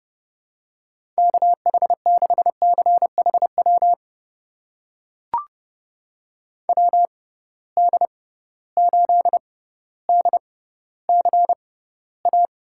Morse code, heard as a keyed tone, about 30 wpm, spaced as for 15 wpm, 700 Hz. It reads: KH6CHW WD8DCA
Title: USA Call Signs - 6 Characters - 6x Character Spacing - 30wpm